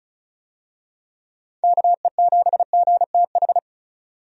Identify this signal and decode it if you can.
Morse code, heard as a keyed tone, about 36 wpm, 700 Hz. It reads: KE7GTH
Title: USA Call Signs - 6 Characters - 35wpm